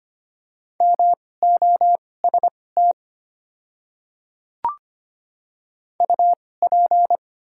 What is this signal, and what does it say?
Morse code, heard as a keyed tone, about 25 wpm, spaced as for 18 wpm, 700 Hz. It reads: MOST UP
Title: 2x Character Spacing - Top 100 Words - 25wpm